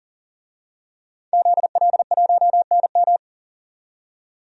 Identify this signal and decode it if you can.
Morse code, heard as a keyed tone, about 40 wpm, 700 Hz. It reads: ZL1NM
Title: DXCC-100 Call Signs - 5 Characters - 40wpm